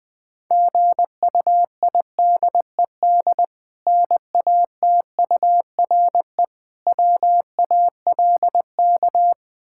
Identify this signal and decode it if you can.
Morse code, heard as a keyed tone, about 21 wpm, 700 Hz. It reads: GUIDED NATURE WALK
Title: Difficult Words - Speed-Racing with Hints - 20wpm